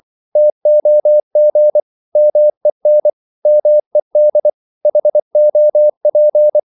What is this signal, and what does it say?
Morse code, heard as a keyed tone, about 24 wpm, 600 Hz. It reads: TOG MEN MED HOP
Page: Korte danske ord 24wpm